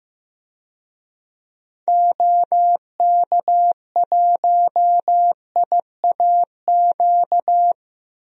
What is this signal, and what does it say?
Morse code, heard as a keyed tone, about 15 wpm, 700 Hz. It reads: OK1IAQ
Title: DXCC-100 Call Signs - 6 Characters - 15wpm